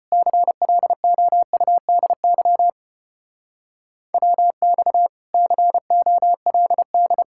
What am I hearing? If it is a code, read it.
Morse code, heard as a keyed tone, about 33 wpm, 700 Hz. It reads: CLOUDY WX COLD